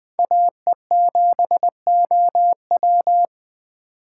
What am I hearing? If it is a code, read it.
Morse code, heard as a keyed tone, about 20 wpm, 700 Hz. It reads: AE7OW
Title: USA Call Signs - 5 Characters - 2x Character Spacing - 20wpm